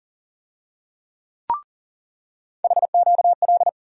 Morse code, heard as a keyed tone, about 40 wpm, 700 Hz. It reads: HQL
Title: USA Call Sign Postfixes - 3 Letters 40wpm